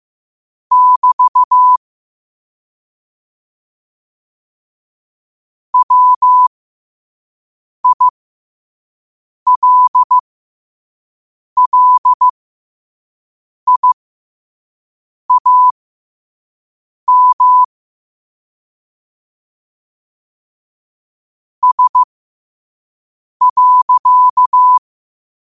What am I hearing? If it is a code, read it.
Morse code, heard as a keyed tone, about 15 wpm, spaced as for 5 wpm, 1000 Hz. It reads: <BT> WILLIAM S.
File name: qotdfiles / 5wpm